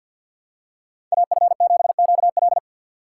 Morse code, heard as a keyed tone, about 49 wpm, 700 Hz. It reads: AF7QL
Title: USA Call Signs - 2x Character Spacing - 50wpm